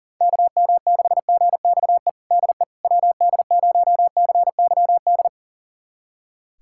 Morse code, heard as a keyed tone, about 40 wpm, 700 Hz. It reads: KM6GXE DE WD0CYD
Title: USA Callsign DE Callsign - 6 Characters - 40wpm